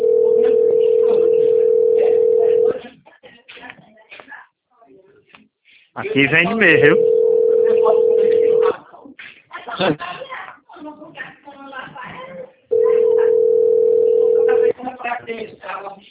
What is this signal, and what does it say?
Ringback tone (North America), 440 + 480 Hz, with speech.
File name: rg-1600-unknown- - - .10657